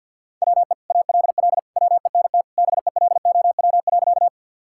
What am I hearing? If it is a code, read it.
Morse code, heard as a keyed tone, about 53 wpm, 700 Hz. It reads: WE ALL WENT BELOW.
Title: Sentences from Top 1000 Words - Review Part 5 - 50wpm